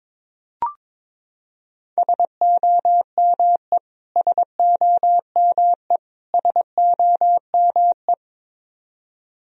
Morse code, heard as a keyed tone, about 22 wpm, 700 Hz. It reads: SOME SOME SOME
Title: Top 100 Words Repeated 22wpm